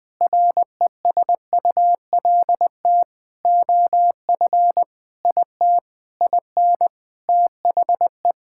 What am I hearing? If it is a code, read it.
Morse code, heard as a keyed tone, about 20 wpm, 700 Hz. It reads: RESULT OF IT IN THE
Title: Sentences from Top 1000 Words - Review Part 5 - 20wpm